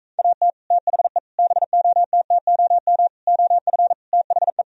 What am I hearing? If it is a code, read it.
Morse code, heard as a keyed tone, about 45 wpm, 700 Hz. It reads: AT THE BOTTOM OF THE